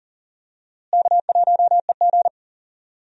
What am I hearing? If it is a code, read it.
Morse code, heard as a keyed tone, about 40 wpm, 700 Hz. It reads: K1EG
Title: USA Call Signs - 4 Characters - Repeated 3x - 40wpm